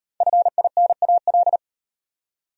Morse code, heard as a keyed tone, about 33 wpm, 700 Hz. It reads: FINAL